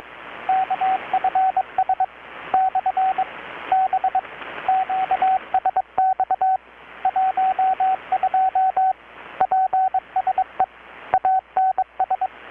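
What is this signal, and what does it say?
Morse code, heard as a keyed tone, about 22 wpm, 760 Hz, with radio noise. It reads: KFS / B QSX 12 PSE ANS
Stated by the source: Morse from KPH's Night of Nights XX maritime radiotelegraphy event, received in Chiapas, Mexico about 3,800 km away on a Kenwood R-600 with a grounded 10 m long-wire antenna.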